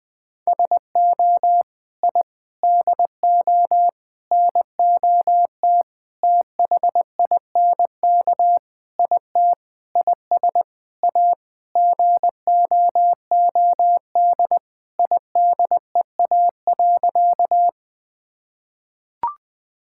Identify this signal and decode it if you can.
Morse code, heard as a keyed tone, about 20 wpm, 700 Hz. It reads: SO I DO NOT THINK IT IS A GOOD IDEA.
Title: Sentences with 5-Word Phrases - 20wpm